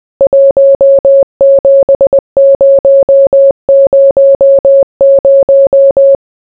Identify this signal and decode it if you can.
Morse code, heard as a keyed tone, about 20 wpm, 550 Hz. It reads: 17000